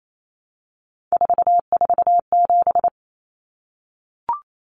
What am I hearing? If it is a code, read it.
Morse code, heard as a keyed tone, about 28 wpm, 700 Hz. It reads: 447
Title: Common Cut Numbers - 3-Digit 28wpm